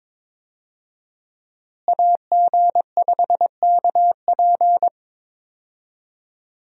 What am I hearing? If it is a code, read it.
Morse code, heard as a keyed tone, about 21 wpm, 700 Hz. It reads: AG5KP
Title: USA Call Signs - 5 Characters - Repeated 2x - 22wpm